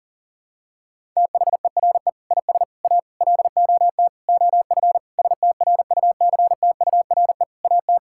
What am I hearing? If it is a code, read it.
Morse code, heard as a keyed tone, about 37 wpm, 700 Hz. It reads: THERE IS A LOT OF STRUCTURE AT
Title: Sentences from Top 1000 Words - Review Part 6 - 40wpm